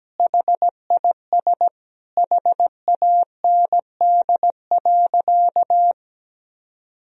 Morse code, heard as a keyed tone, about 17 wpm, 700 Hz. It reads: HIS HAND.
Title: Sentences from Top 1000 Words - Review Part 4 - 17wpm